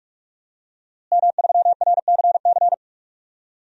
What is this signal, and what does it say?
Morse code, heard as a keyed tone, about 46 wpm, 700 Hz. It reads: M3RCC
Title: DXCC-100 Call Signs - 5 Characters - Repeated 2x - 45wpm